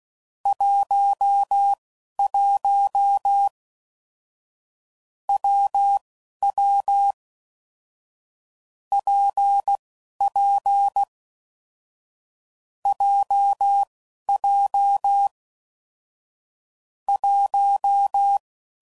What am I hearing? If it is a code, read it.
Morse code, heard as a keyed tone, about 16 wpm, spaced as for 11 wpm, 800 Hz. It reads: 11 WW PP JJ 1
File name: Combinations / Difficult Combos / WPJ WPJ1